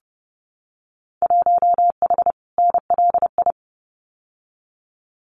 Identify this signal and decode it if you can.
Morse code, heard as a keyed tone, about 30 wpm, 700 Hz. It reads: 1H NLI
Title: Winter Field Day Exchanges 30wpm